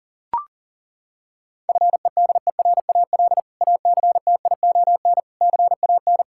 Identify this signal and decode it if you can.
Morse code, heard as a keyed tone, about 40 wpm, 700 Hz. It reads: FEDERAL ACTION CAN
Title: Sentences from Top 1000 Words - Review Part 5 - 40wpm